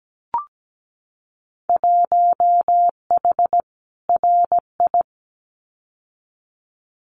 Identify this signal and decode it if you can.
Morse code, heard as a keyed tone, about 17 wpm, 700 Hz. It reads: 1H RI
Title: Winter Field Day Exchanges 17wpm